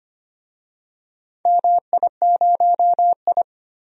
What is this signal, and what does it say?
Morse code, heard as a keyed tone, about 25 wpm, 700 Hz. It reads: MI0I
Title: DXCC-100 Call Signs - 4 Characters - 25wpm